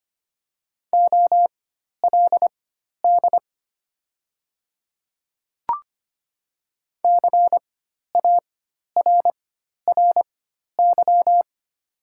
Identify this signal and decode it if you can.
Morse code, heard as a keyed tone, about 25 wpm, spaced as for 12 wpm, 700 Hz. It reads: OLD CARRY
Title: 4x Character Spacing - Top 300 Words - 25wpm